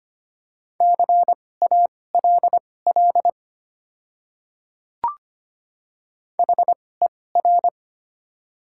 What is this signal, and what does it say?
Morse code, heard as a keyed tone, about 25 wpm, spaced as for 18 wpm, 700 Hz. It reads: CALL HER
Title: 2x Character Spacing - Top 300 Words - 25wpm